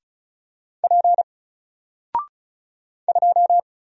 Morse code, heard as a keyed tone, about 35 wpm, 700 Hz. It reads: P 2
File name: Single-Letter-Number-35wpm